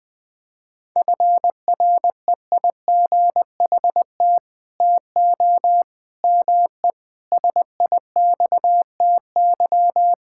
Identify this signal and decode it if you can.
Morse code, heard as a keyed tone, about 19 wpm, 700 Hz. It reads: FREIGHT TO ME SIXTY
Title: Pangram Challenge v2 - 20wpm